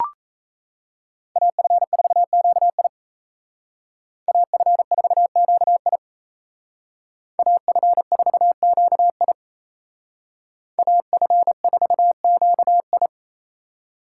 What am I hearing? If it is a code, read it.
Morse code, heard as a keyed tone, about 36 wpm, 700 Hz. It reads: AF4QI AF4QI AF4QI AF4QI